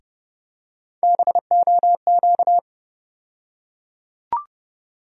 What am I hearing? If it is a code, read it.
Morse code, heard as a keyed tone, about 30 wpm, 700 Hz. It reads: BOQ
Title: USA Call Sign Postfixes - 3 Letters 30wpm